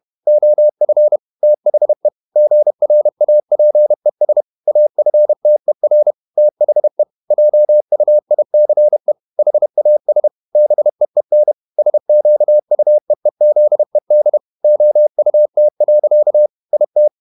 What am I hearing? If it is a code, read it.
Morse code, heard as a keyed tone, about 31 wpm, 600 Hz. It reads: OF THE GRAPES AFTER THE JUICE HAS BEEN SQUEEZED OUT. IT
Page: Never 31wpm | CW med Gnister